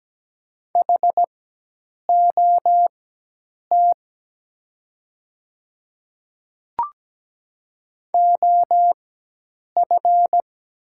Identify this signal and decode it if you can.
Morse code, heard as a keyed tone, about 17 wpm, spaced as for 8 wpm, 700 Hz. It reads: HOT OF
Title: 4x Character Spacing - Top 100 Words - 17wpm